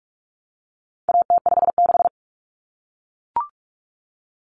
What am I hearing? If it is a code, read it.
Morse code, heard as a keyed tone, about 44 wpm, 700 Hz. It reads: AT56
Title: Common Cut Numbers - 4-Digit 45wpm